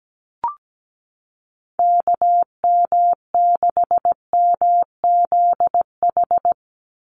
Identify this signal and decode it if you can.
Morse code, heard as a keyed tone, about 17 wpm, 700 Hz. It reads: KM6MZH
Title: USA Call Signs 17wpm